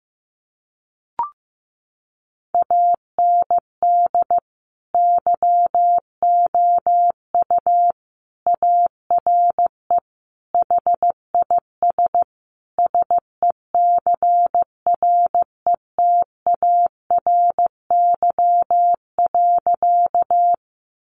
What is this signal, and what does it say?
Morse code, heard as a keyed tone, about 14 wpm, 700 Hz. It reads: AND YOU ARE HIS SECRETARY.
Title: Sentences from Top 1300 Words 15wpm